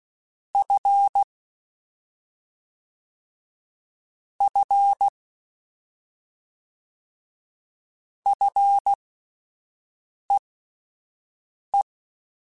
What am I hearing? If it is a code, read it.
Morse code, heard as a keyed tone, about 16 wpm, spaced as for 5 wpm, 800 Hz. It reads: F F FEE